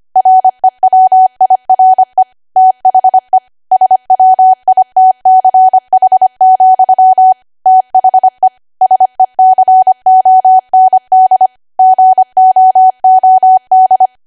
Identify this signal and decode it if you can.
Morse code, heard as a keyed tone, about 25 wpm, 750 Hz. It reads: REWIRE THE SWITCH, THE SECOND GOOD
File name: Archive / 25 WPM / 25WPM